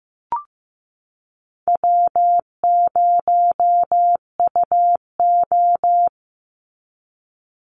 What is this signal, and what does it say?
Morse code, heard as a keyed tone, about 15 wpm, 700 Hz. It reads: W0UO
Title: Easy Call Signs Repeated 2x - 15wpm